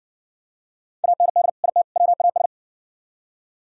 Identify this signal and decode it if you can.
Morse code, heard as a keyed tone, about 60 wpm, 700 Hz. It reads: AND IT WAS